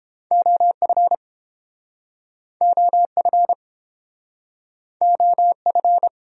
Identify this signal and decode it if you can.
Morse code, heard as a keyed tone, about 32 wpm, 700 Hz. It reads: OF OF OF